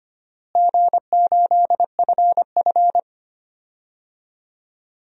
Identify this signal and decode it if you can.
Morse code, heard as a keyed tone, about 25 wpm, 700 Hz. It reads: G8FF
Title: DXCC-100 Call Signs - 4 Characters - Repeated 2x - 25wpm